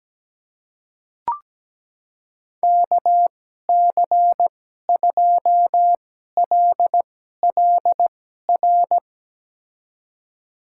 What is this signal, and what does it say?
Morse code, heard as a keyed tone, about 17 wpm, spaced as for 12 wpm, 700 Hz. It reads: KC2LLR
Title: USA Call Signs - 6 Characters - 2x Character Spacing - 17wpm